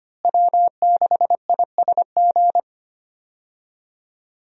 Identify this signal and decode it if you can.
Morse code, heard as a keyed tone, about 25 wpm, 700 Hz. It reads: W6ISG
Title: USA Call Signs - 5 Characters - Repeated 2x - 25wpm